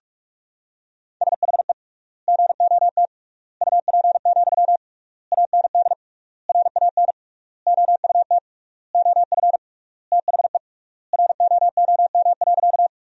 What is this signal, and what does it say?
Morse code, heard as a keyed tone, about 46 wpm, 700 Hz. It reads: SHE GOT UP, AND RAN OUT OF THE ROOM.
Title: 3x Word-Spacing - Sentences from Top 500 Words - 45wpm